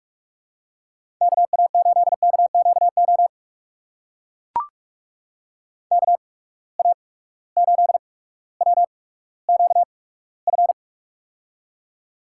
Morse code, heard as a keyed tone, about 42 wpm, spaced as for 19 wpm, 700 Hz. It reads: KA8KQO KA8WQF
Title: USA Call Signs - 6 Characters - 8x Character Spacing - 45wpm